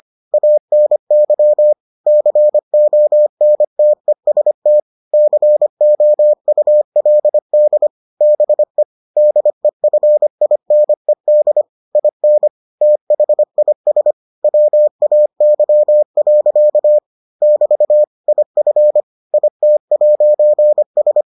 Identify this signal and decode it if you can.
Morse code, heard as a keyed tone, about 25 wpm, 600 Hz. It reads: ANY CONTEST COULD BE DEFINED IN THIS WAY. <BT> IF IT'S